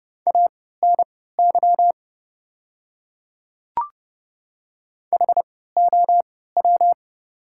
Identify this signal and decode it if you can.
Morse code, heard as a keyed tone, about 30 wpm, spaced as for 17 wpm, 690 Hz. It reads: ANY HOW
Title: 3x Character Spacing - Top 200 Words - 30wpm